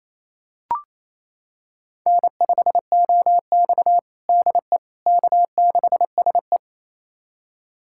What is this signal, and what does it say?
Morse code, heard as a keyed tone, about 28 wpm, 700 Hz. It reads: N5OX DE K6SE
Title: USA Callsign DE Callsign - 4 Characters - 28wpm